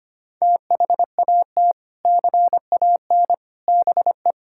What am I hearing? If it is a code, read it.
Morse code, heard as a keyed tone, about 25 wpm, 700 Hz. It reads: THAT CAN BE
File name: practice / 25wpm / Sentences with 3-Word Phrases - 25wpm